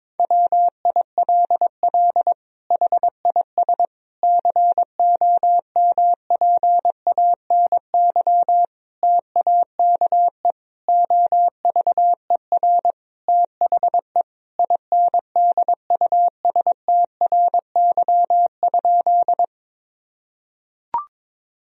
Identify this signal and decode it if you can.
Morse code, heard as a keyed tone, about 21 wpm, 700 Hz. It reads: WILL HIS COMPANY TAKE OVER THE INDUSTRY?
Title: Sentences from Top 1000 Words - Review Part 4 - 22wpm